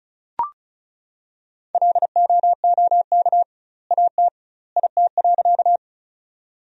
Morse code, heard as a keyed tone, about 36 wpm, 700 Hz. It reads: LOOK AT IT.
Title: 2x Word-Spacing - Sentences from Top 100 Words - 35wpm